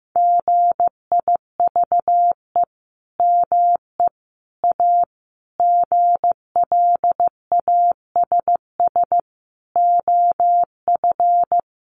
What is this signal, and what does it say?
Morse code, heard as a keyed tone, about 15 wpm, 700 Hz. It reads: GIVE ME A GLASS OF
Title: Sentences from Top 1500 Words 15wpm